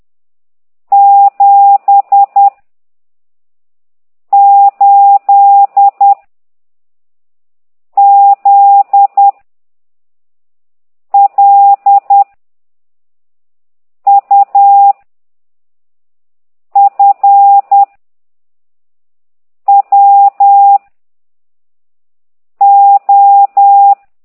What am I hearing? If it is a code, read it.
Morse code, heard as a keyed tone, about 10 wpm, 800 Hz. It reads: 7 8 Z L U F W O
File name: ct2idw / exerc15